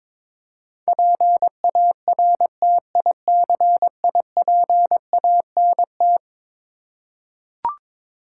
Morse code, heard as a keyed tone, about 22 wpm, 700 Hz. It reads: PARTICIPANT
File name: practice / 22wpm / Eleven Letter Words - 22wpm